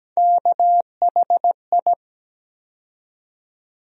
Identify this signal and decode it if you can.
Morse code, heard as a keyed tone, about 17 wpm, 700 Hz. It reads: KHI
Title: USA Call Sign Postfixes - 3 Letters 17wpm